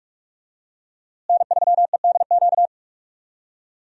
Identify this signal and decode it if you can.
Morse code, heard as a keyed tone, about 43 wpm, 700 Hz. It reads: N3EDQ
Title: USA Call Signs - 5 Characters - Repeated 2x - 45wpm